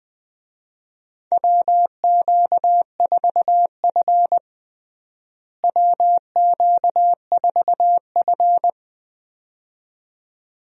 Morse code, heard as a keyed tone, about 20 wpm, 700 Hz. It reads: WQ4F WQ4F
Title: USA Call Signs - 4 Characters - Repeated 2x - 20wpm